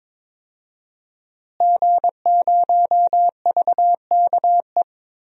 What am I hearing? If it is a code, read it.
Morse code, heard as a keyed tone, about 22 wpm, 700 Hz. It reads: G0VKE
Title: DXCC-100 Call Signs - 5 Characters - Repeated 3x - 22wpm